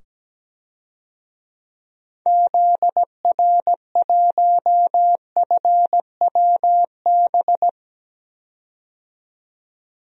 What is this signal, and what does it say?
Morse code, heard as a keyed tone, about 17 wpm, 700 Hz. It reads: ZR1FWB
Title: DXCC-100 Call Signs - 6 Characters - 17wpm